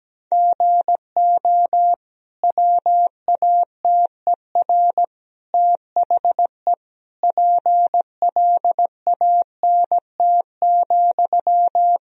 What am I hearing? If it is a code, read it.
Morse code, heard as a keyed tone, about 17 wpm, 700 Hz. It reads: GO WATER THE PLANT,
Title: Sentences from Top 1000 Words - Review Part 1 - 17wpm